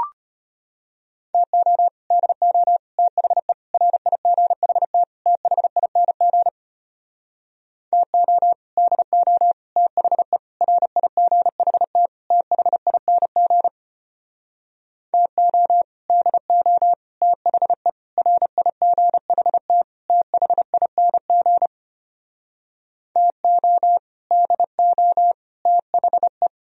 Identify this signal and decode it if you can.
Morse code, heard as a keyed tone, about 33 wpm, 700 Hz. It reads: TO DO THE RIGHT THING TO DO THE RIGHT THING TO DO THE RIGHT THING TO DO THE